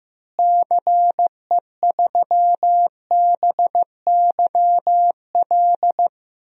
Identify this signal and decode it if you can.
Morse code, heard as a keyed tone, about 15 wpm, 700 Hz. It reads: CE3BYL
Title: DXCC-100 Call Signs - 6 Characters - Repeated 3x - 15wpm